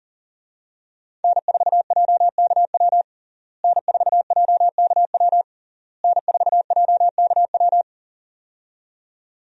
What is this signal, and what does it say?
Morse code, heard as a keyed tone, about 40 wpm, 700 Hz. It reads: N4JKW N4JKW N4JKW
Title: USA Call Signs - 5 Characters - Repeated 3x - 40wpm